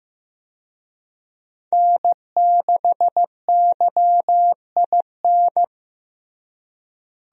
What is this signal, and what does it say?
Morse code, heard as a keyed tone, about 15 wpm, 700 Hz. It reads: N6YIN
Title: USA Call Signs - 5 Characters - Repeated 2x - 15wpm